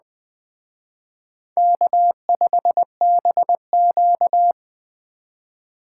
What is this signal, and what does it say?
Morse code, heard as a keyed tone, about 20 wpm, 700 Hz. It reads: K5BQ